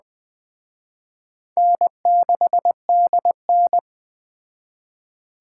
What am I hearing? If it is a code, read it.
Morse code, heard as a keyed tone, about 21 wpm, 700 Hz. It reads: N6DN
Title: Speed Racing - USA 1x2 Call Signs 17wpm